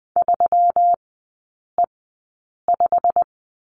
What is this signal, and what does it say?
Morse code, heard as a keyed tone, about 19 wpm, spaced as for 8 wpm, 700 Hz. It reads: 3E5
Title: 121 Lesson 20e - Number 3 - Three Characters 20wpm